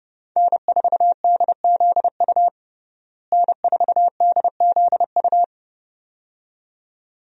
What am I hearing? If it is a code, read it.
Morse code, heard as a keyed tone, about 32 wpm, 700 Hz. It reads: N4DZU N4DZU
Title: USA Call Signs - 5 Characters - Repeated 2x - 30wpm